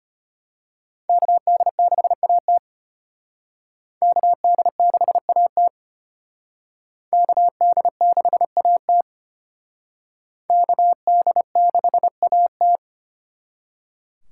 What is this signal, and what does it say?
Morse code, heard as a keyed tone, about 33 wpm, 700 Hz. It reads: KD6AT KD6AT KD6AT KD6AT